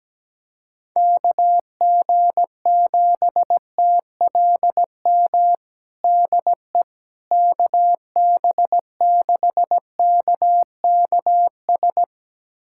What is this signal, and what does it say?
Morse code, heard as a keyed tone, about 17 wpm, 700 Hz. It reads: KG7TLM DE KB6KKS
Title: USA Callsign DE Callsign - 6 Characters - 17wpm